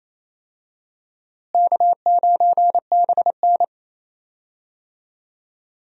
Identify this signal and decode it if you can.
Morse code, heard as a keyed tone, about 27 wpm, 700 Hz. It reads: K9BN
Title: USA Call Signs - 4 Characters - Repeated 2x - 28wpm